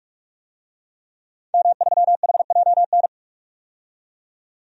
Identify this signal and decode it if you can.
Morse code, heard as a keyed tone, about 46 wpm, 700 Hz. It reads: M3HJN
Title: DXCC-100 Call Signs - 5 Characters - 45wpm